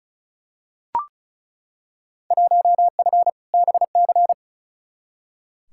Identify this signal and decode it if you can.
Morse code, heard as a keyed tone, about 36 wpm, 700 Hz. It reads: 1F BC
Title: Field Day Exchanges Encore - 35wpm